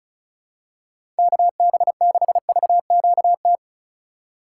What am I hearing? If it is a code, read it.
Morse code, heard as a keyed tone, about 34 wpm, 700 Hz. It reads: KB6VQT
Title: USA Call Signs - 6 Characters - 3x Character Spacing - 35wpm